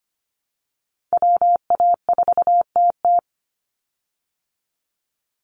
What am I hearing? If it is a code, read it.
Morse code, heard as a keyed tone, about 25 wpm, 700 Hz. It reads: WA4TT
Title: Call Signs Repeated 3x - 25wpm